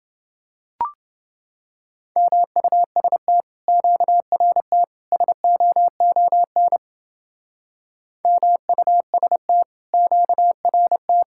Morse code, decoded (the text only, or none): MUST QRT SOON MUST QRT